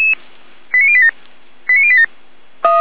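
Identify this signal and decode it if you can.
Five-tone selective call (ZVEI), heard as quick runs of five tones, tones of about 70 ms. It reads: 89087 89087
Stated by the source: Five-tone ZVEI selective call received by SDR.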